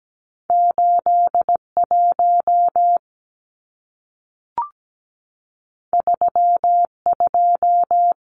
Morse code, heard as a keyed tone, about 17 wpm, 700 Hz. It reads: 81 32
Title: Numbers - 2-Digit 17wpm